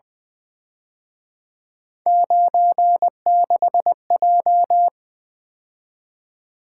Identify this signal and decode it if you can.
Morse code, heard as a keyed tone, about 24 wpm, 700 Hz. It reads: 96J
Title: Speed-Racing - Three Letter-Number Combinations - 20wpm